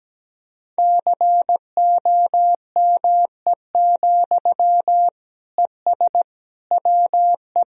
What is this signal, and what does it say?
Morse code, heard as a keyed tone, about 17 wpm, 700 Hz. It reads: COME, ES WE
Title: Sentences from Top 500 Words and Abbreviations 17wpm